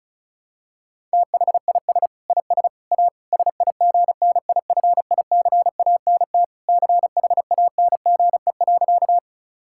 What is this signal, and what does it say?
Morse code, heard as a keyed tone, about 36 wpm, 700 Hz. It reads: THIS IS A SIGNIFICANT CHANGE.
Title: Sentences from Top 1000 Words - Review Part 6 - 35wpm